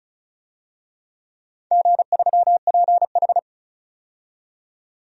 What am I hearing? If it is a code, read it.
Morse code, heard as a keyed tone, about 36 wpm, 700 Hz. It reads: G3PH
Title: DXCC-100 Call Signs - 4 Characters - 35wpm